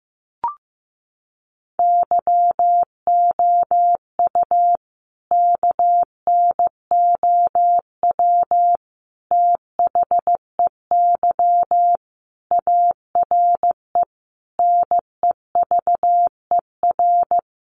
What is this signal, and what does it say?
Morse code, heard as a keyed tone, about 15 wpm, 700 Hz. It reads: YOU KNOW THEY ARE NEVER
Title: Sentences from Top 1400 Words 15wpm